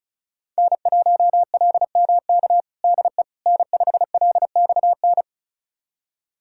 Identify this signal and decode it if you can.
Morse code, heard as a keyed tone, about 36 wpm, 700 Hz. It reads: N1LMK DE N5LXN
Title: USA Callsign DE Callsign - 5 Characters - 35wpm